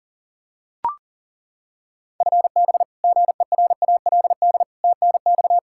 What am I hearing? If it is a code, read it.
Morse code, heard as a keyed tone, about 40 wpm, 700 Hz. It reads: FB GERALD TNX
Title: Practice QSO - Exchange 3 - More Info - 40wpm